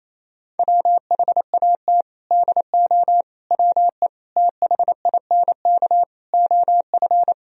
Morse code, heard as a keyed tone, about 27 wpm, 700 Hz. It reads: WHAT DO WE THINK OF
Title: Sentences from Top 1000 Words - Review Part 2 - 28wpm